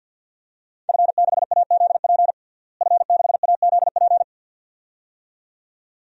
Morse code, heard as a keyed tone, about 53 wpm, 700 Hz. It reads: F6AZP F6AZP
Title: DXCC-100 Call Signs - 5 Characters - Repeated 2x - 50wpm